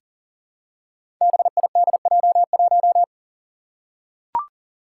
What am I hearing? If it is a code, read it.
Morse code, heard as a keyed tone, about 40 wpm, 700 Hz. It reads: BIDJ1
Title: Five Challenging Letter-Number Combinations 40wpm